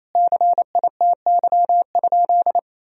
Morse code, heard as a keyed tone, about 28 wpm, 700 Hz. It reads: CITY?
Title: Sentences from Top 1000 Words - Review Part 1 - 28wpm